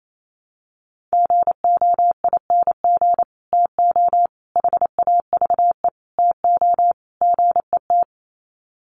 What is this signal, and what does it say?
Morse code, heard as a keyed tone, about 28 wpm, 700 Hz. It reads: GOING TO HAVE TO GET
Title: Sets of 5 Words Encore2 28wpm